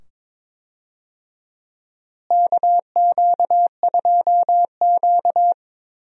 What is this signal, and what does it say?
Morse code, heard as a keyed tone, about 22 wpm, 700 Hz. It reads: KQ2Q
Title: USA Call Signs - 4 Characters - Repeated 2x - 22wpm